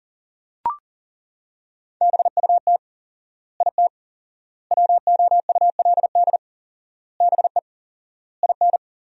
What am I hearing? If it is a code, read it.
Morse code, heard as a keyed tone, about 40 wpm, 700 Hz. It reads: BUT IT WOULD BE IN
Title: 4x Word-Spacing - Sentences from Top 300 Words - 40wpm